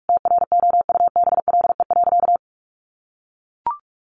Morse code, heard as a keyed tone, about 44 wpm, 700 Hz. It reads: TROUBLE.
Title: Sentences from Top 1200 Words 45wpm